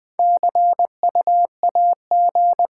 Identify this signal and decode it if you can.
Morse code, heard as a keyed tone, about 20 wpm, 700 Hz. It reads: CU AG
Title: Top 20 QSO Elements - 6x Character Spacing - 20wpm